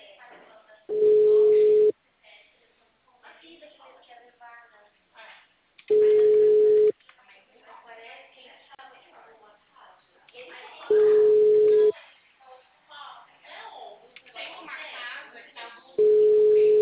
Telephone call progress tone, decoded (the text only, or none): ringback tone